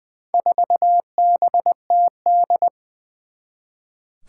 Morse code, heard as a keyed tone, about 21 wpm, 700 Hz. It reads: 4BTD
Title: Four Challenging Letter-Number Combinations 20wpm